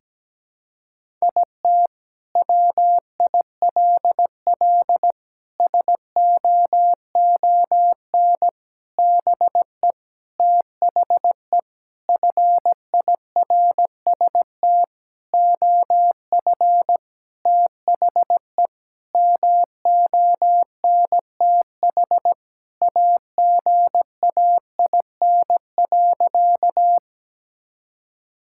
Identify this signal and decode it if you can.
Morse code, heard as a keyed tone, about 17 wpm, 700 Hz. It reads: IT WILL SOON BE THE FIRST OF THE MONTH AGAIN.
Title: Sentences from Top 1000 Words - Review Part 5 - 17wpm